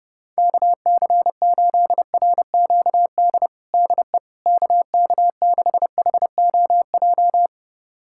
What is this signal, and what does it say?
Morse code, heard as a keyed tone, about 30 wpm, 700 Hz. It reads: KC8RQD DE KK6HOJ
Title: USA Callsign DE Callsign - 6 Characters - 30wpm